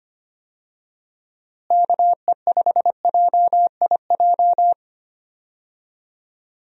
Morse code, heard as a keyed tone, about 25 wpm, 700 Hz. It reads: KE5JIJ
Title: USA Call Signs - 6 Characters - Repeated 3x - 25wpm